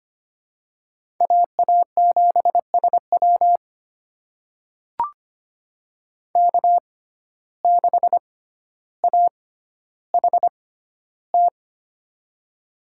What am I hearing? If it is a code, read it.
Morse code, heard as a keyed tone, about 25 wpm, spaced as for 13 wpm, 700 Hz. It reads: AA7SW K6AHT
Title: USA Call Signs - 5 Characters - 6x Character Spacing - 25wpm